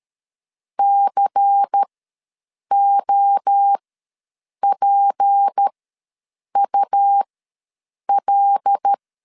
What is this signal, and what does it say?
Morse code, heard as a keyed tone, about 13 wpm, spaced as for 7 wpm, 800 Hz. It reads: COPUL
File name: LEZIONI / 6 PKX0 / 6 P ws 12wpm 3117